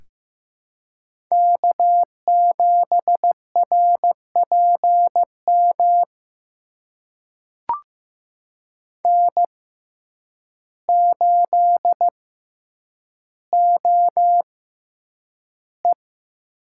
Morse code, heard as a keyed tone, about 15 wpm, spaced as for 8 wpm, 700 Hz. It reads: K7RPM N8OE
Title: USA Call Signs - 5 Characters - 6x Character Spacing - 15wpm